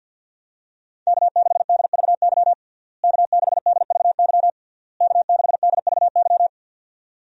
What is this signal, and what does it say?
Morse code, heard as a keyed tone, about 52 wpm, 700 Hz. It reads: K6DVY K6DVY K6DVY
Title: USA Call Signs - 5 Characters - Repeated 3x - 50wpm